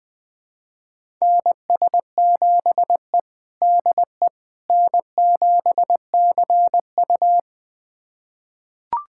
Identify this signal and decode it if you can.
Morse code, heard as a keyed tone, about 21 wpm, 700 Hz. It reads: NS7E DE N7CU
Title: USA Callsign DE Callsign - 4 Characters - 20wpm